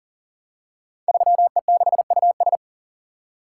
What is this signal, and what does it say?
Morse code, heard as a keyed tone, about 42 wpm, 700 Hz. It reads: 3E6US
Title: Code Groups - 5 Letter:Numbers - 40wpm